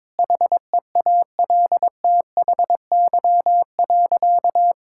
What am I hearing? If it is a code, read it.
Morse code, heard as a keyed tone, about 22 wpm, 700 Hz. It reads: HEALTHY.
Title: Sentences from Top 1000 Words - Review Part 5 - 22wpm